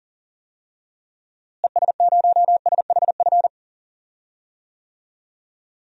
Morse code, heard as a keyed tone, about 40 wpm, 700 Hz. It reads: ES0SHF
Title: DXCC-100 Call Signs - 6 Characters - Repeated 3x - 40wpm